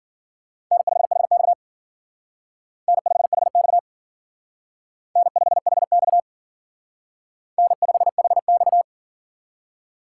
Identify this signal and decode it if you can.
Morse code, heard as a keyed tone, about 52 wpm, 700 Hz. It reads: N5HX N5HX N5HX N5HX